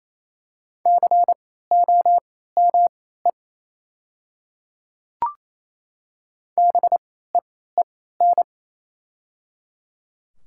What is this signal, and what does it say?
Morse code, heard as a keyed tone, about 28 wpm, spaced as for 16 wpm, 700 Hz. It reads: COME BEEN
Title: 3x Character Spacing - Top 100 Words - 28wpm